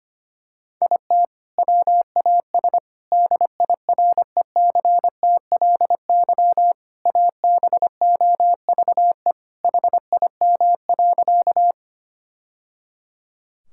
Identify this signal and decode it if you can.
Morse code, heard as a keyed tone, about 25 wpm, 700 Hz. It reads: IT WAS DIRECTLY ABOVE HIM.
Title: Sentences from Top 1000 Words - Review Part 5 - 25wpm